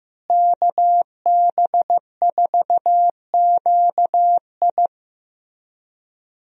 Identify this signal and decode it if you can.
Morse code, heard as a keyed tone, about 15 wpm, 700 Hz. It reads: KB4QI
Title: USA Call Signs - 5 Characters - Repeated 3x - 15wpm